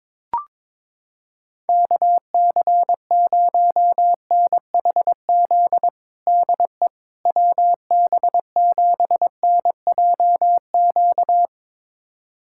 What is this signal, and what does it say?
Morse code, heard as a keyed tone, about 22 wpm, 700 Hz. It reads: KC0NHZ DE WB7NJQ
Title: USA Callsign DE Callsign - 6 Characters - 22wpm